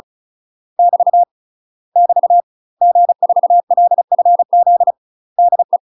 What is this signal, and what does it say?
Morse code, heard as a keyed tone, about 34 wpm, 700 Hz. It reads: <BT> <BT> G4LFZ DE
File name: UK MASTER QSOs TRAINING / 35WPM / UK MASTER QSOs 0003